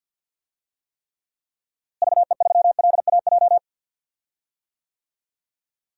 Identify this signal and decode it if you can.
Morse code, heard as a keyed tone, about 47 wpm, 700 Hz. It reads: VE3LAJ
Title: DXCC-100 Call Signs - 6 Characters - Repeated 2x - 50wpm